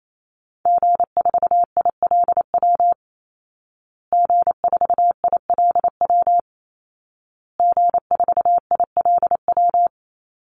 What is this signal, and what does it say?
Morse code, heard as a keyed tone, about 28 wpm, 700 Hz. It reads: G4ILW G4ILW G4ILW
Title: Call Signs Repeated 2x - 28wpm